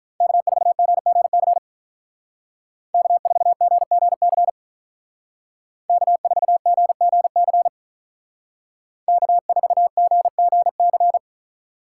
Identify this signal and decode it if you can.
Morse code, heard as a keyed tone, about 44 wpm, 700 Hz. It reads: K4GGC K4GGC K4GGC K4GGC
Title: Speed Racing - USA 1x3 Call Signs 35wpm